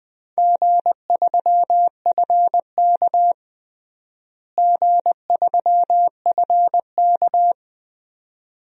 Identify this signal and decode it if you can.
Morse code, heard as a keyed tone, about 20 wpm, 700 Hz. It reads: G3FK G3FK
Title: DXCC-100 Call Signs - 4 Characters - Repeated 2x - 20wpm